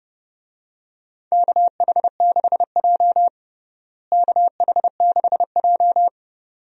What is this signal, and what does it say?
Morse code, heard as a keyed tone, about 32 wpm, 700 Hz. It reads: KH6J KH6J
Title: DXCC-100 Call Signs - 4 Characters - Repeated 2x - 30wpm